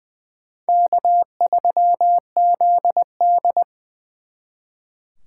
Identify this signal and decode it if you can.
Morse code, heard as a keyed tone, about 23 wpm, 700 Hz. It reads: K3ZD
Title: Speed Racing - USA 1x2 Call Signs 20wpm